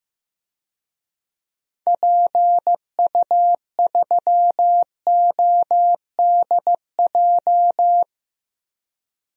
Morse code, heard as a keyed tone, about 15 wpm, 700 Hz. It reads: PU3ODJ
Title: DXCC-100 Call Signs - 6 Characters - Repeated 3x - 15wpm